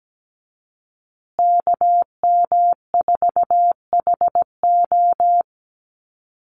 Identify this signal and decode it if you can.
Morse code, heard as a keyed tone, about 17 wpm, 700 Hz. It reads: KM4HO
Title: USA Call Signs Repeated 3x 17wpm